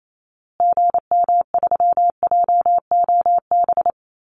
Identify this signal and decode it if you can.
Morse code, heard as a keyed tone, about 28 wpm, 700 Hz. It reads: GM3JOB
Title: Call Signs Repeated 2x - 28wpm